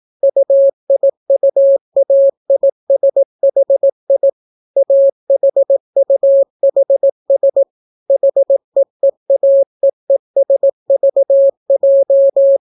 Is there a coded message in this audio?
Yes: Morse code from 0.2 to 12.6 s, heard as a keyed tone, about 18 wpm, 550 Hz. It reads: UIUAISHI AHUHS HEEAEESVJ